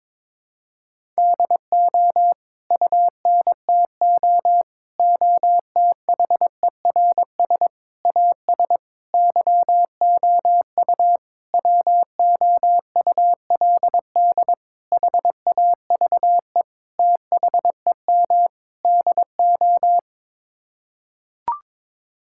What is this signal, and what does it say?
Morse code, heard as a keyed tone, about 22 wpm, 700 Hz. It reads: DO UNTO OTHERS AS YOU WOULD HAVE THEM DO
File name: Sets of 9 Words - 22wpm